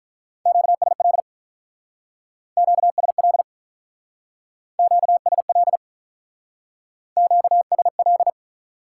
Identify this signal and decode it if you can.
Morse code, heard as a keyed tone, about 44 wpm, 700 Hz. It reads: QSL QSL QSL QSL
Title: Speed-Racing Top 180 QSO Elements 35wpm